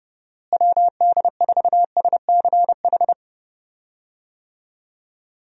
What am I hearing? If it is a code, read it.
Morse code, heard as a keyed tone, about 32 wpm, 700 Hz. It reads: WD4SCH
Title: USA Call Signs - 6 Characters - Repeated 2x - 30wpm